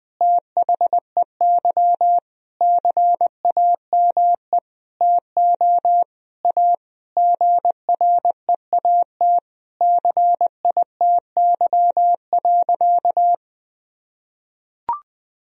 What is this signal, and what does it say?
Morse code, heard as a keyed tone, about 20 wpm, 700 Hz. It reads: THEY CAME TO A GREAT CITY.
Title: Sentences from Top 1000 Words - Review Part 1 - 20wpm